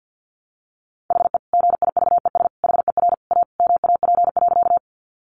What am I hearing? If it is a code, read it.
Morse code, heard as a keyed tone, about 50 wpm, 700 Hz. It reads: HE GIVES HER A MAP.
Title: Additional Easy Phrases 50wpm